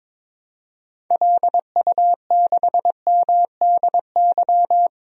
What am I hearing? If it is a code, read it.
Morse code, heard as a keyed tone, about 22 wpm, 700 Hz. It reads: LU6MDY
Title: DXCC-100 Call Signs - 6 Characters - Repeated 3x - 22wpm